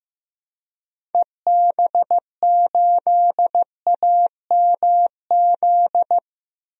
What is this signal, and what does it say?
Morse code, heard as a keyed tone, about 15 wpm, 700 Hz. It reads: EB8AMZ
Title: DXCC-100 Call Signs - 6 Characters - 15wpm